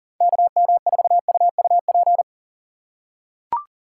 Morse code, heard as a keyed tone, about 40 wpm, 700 Hz. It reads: KM4UUP
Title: USA Call Signs - 6 Characters - 4x Character Spacing - 40wpm